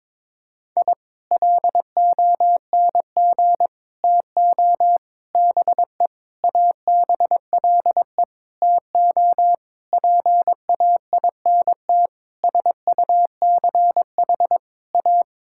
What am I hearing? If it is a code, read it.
Morse code, heard as a keyed tone, about 22 wpm, 700 Hz. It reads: I LONG TO BE ABLE TO PAINT SUCH A
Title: Sentences from Top 1000 Words - Review Part 6 - 22wpm